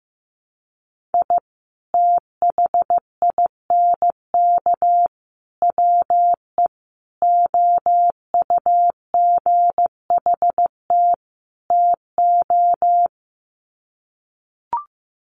Morse code, heard as a keyed tone, about 15 wpm, 700 Hz. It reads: I THINK WE OUGHT TO
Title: Sets of 5 Words Encore 15wpm